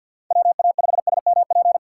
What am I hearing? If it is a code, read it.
Morse code, heard as a keyed tone, about 51 wpm, 700 Hz. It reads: WA5SMP
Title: USA Call Signs - 2x Character Spacing - 50wpm